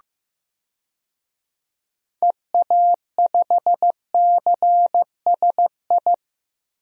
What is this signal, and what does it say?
Morse code, heard as a keyed tone, about 15 wpm, 700 Hz. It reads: EA5CSI
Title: DXCC-100 Call Signs - 6 Characters - 15wpm